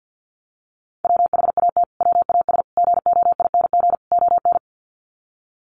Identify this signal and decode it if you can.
Morse code, heard as a keyed tone, about 48 wpm, 700 Hz. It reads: WHAT WAS GOING ON
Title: Sets of 4 Words 50wpm